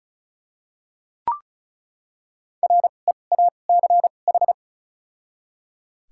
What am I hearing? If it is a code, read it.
Morse code, heard as a keyed tone, about 34 wpm, spaced as for 25 wpm, 700 Hz. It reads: REACH
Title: 2x Character Spacing - Top 500 Words - 35wpm